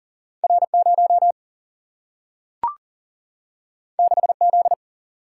Morse code, heard as a keyed tone, about 40 wpm, 700 Hz. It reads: R 0 6 Z
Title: Single-Letter Single-Number 40wpm